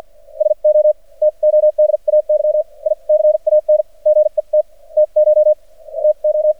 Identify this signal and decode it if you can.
Morse code, heard as a keyed tone, about 53 wpm, spaced as for 42 wpm, 610 Hz. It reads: SO TODAY I CAN GET TCH TO
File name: archived / CRSnet